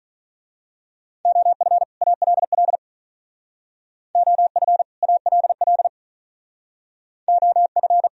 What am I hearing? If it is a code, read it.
Morse code, heard as a keyed tone, about 41 wpm, 700 Hz. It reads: OF ALL OF ALL OF
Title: Speed-Racing - Sets of 2 Words 35wpm